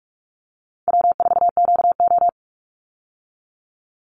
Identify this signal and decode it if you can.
Morse code, heard as a keyed tone, about 44 wpm, 700 Hz. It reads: W4QO
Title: Easy Call Signs Repeated 3x - 45wpm